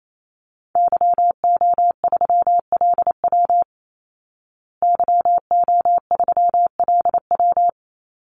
Morse code, heard as a keyed tone, about 28 wpm, 700 Hz. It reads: YO3LW YO3LW
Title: Call Signs Repeated 2x - 28wpm